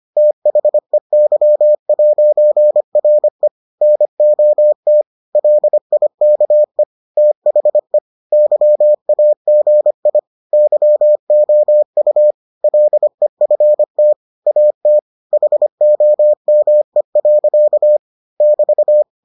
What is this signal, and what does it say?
Morse code, heard as a keyed tone, about 25 wpm, 600 Hz. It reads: THEY'RE NOT LIKE THE YAGI YOU LEFT AT HOME. <BT>